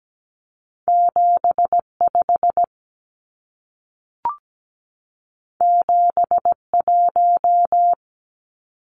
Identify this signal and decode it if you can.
Morse code, heard as a keyed tone, about 17 wpm, 700 Hz. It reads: 75 71
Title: Numbers - 2-Digit 17wpm